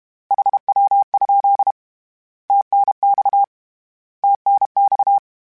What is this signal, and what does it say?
Morse code, heard as a keyed tone, about 32 wpm, 800 Hz. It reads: HW? TNX TNX
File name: Abreviations / Abreviations mp3s / Abreviations@30